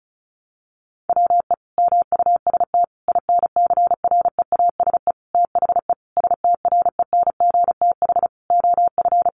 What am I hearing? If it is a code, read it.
Morse code, heard as a keyed tone, about 34 wpm, 700 Hz. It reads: WE MUST INCREASE THE STRENGTH OF
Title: Sentences from Top 1300 Words 35wpm